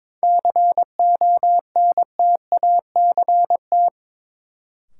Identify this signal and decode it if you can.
Morse code, heard as a keyed tone, about 22 wpm, 700 Hz. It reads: CONTACT
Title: Sentences from Top 1000 Words - Review Part 7 - New Word Drill - 22wpm